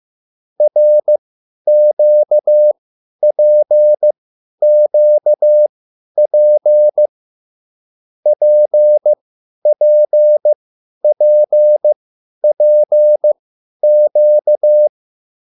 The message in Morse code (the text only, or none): RQPQP PPPPQ